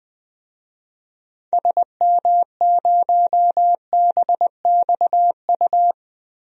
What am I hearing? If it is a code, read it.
Morse code, heard as a keyed tone, about 20 wpm, 700 Hz. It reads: SM0BXU
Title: DXCC-100 Call Signs - 6 Characters - 20wpm